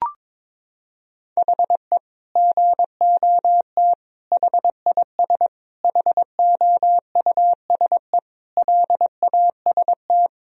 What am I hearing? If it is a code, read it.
Morse code, heard as a keyed tone, about 21 wpm, 700 Hz. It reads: HE GOT HIS HOUSE LAST
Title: Sentences from Top 1000 Words - Review Part 1 - 22wpm